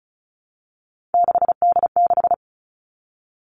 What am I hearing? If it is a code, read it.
Morse code, heard as a keyed tone, about 36 wpm, 700 Hz. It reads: 6D6
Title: Three Challenging Letter-Number Combinations 35wpm